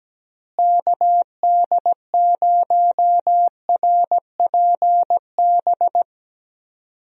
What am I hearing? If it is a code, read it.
Morse code, heard as a keyed tone, about 17 wpm, 700 Hz. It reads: KD0RPB
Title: USA Call Signs - 6 Characters - Repeated 3x - 17wpm